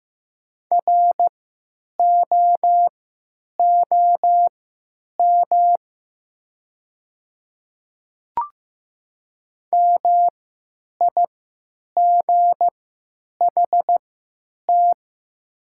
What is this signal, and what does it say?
Morse code, heard as a keyed tone, about 15 wpm, spaced as for 9 wpm, 690 Hz. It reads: ROOM MIGHT
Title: 3x Character Spacing - Top 300 Words - 15wpm